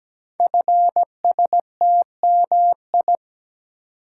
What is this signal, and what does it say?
Morse code, heard as a keyed tone, about 17 wpm, 700 Hz. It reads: FSTMI
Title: Code Groups - 5 Letters - 17wpm